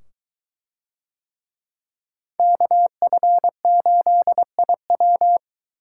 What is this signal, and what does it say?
Morse code, heard as a keyed tone, about 21 wpm, 700 Hz. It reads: KF8IW